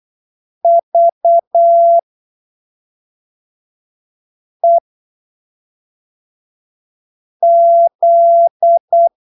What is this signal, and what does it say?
Morse code, heard as a keyed tone, about 8 wpm, spaced as for 3 wpm, 680 Hz. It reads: VEZ
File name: texto 24 08ppm